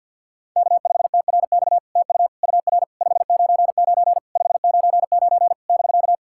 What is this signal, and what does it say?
Morse code, heard as a keyed tone, about 52 wpm, 690 Hz. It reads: K5TRX TU UR 599 599 <BK>
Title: SOTA QSO Practice 50wpm